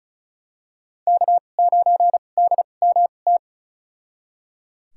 Morse code, heard as a keyed tone, about 34 wpm, spaced as for 25 wpm, 700 Hz. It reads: K9DMT
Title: USA Call Signs - 5 Characters - 2x Character Spacing - 35wpm